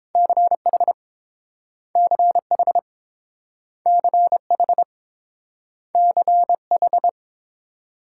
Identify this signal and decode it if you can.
Morse code, heard as a keyed tone, about 30 wpm, 700 Hz. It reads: CH CH CH CH